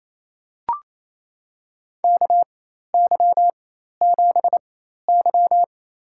Morse code, heard as a keyed tone, about 28 wpm, spaced as for 13 wpm, 700 Hz. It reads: KY7Y
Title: USA Call Signs - 4 Characters - 4x Character Spacing - 28wpm